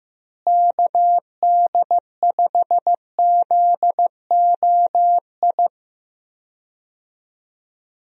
Morse code, heard as a keyed tone, about 16 wpm, 700 Hz. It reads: KD5ZOI
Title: USA Call Signs - 6 Characters - Repeated 3x - 15wpm